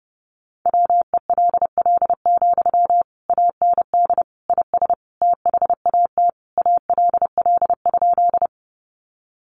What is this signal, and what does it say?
Morse code, heard as a keyed tone, about 30 wpm, 700 Hz. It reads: WELL, AND IS THAT ALL?
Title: Sentences from Top 200 Words 30wpm